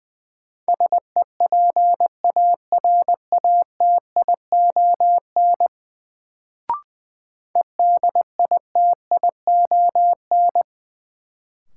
Morse code, heard as a keyed tone, about 20 wpm, 700 Hz. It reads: SEPARATION EDITION
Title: Sets of 5 Words - New Word Drill - 20wpm